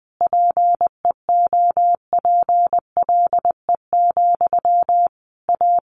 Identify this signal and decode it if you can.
Morse code, heard as a keyed tone, about 20 wpm, 700 Hz. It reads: PEOPLE, A
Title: Sentences from Top 1500 Words 20wpm